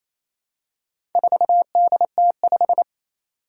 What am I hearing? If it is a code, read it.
Morse code, heard as a keyed tone, about 28 wpm, 700 Hz. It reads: 4DT5
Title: Four Challenging Letter-Number Combinations 28wpm